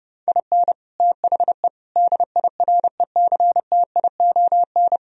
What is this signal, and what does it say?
Morse code, heard as a keyed tone, about 36 wpm, 700 Hz. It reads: IN THE DIRECTION
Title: Speed-Racing - Sets of 4 Words 30wpm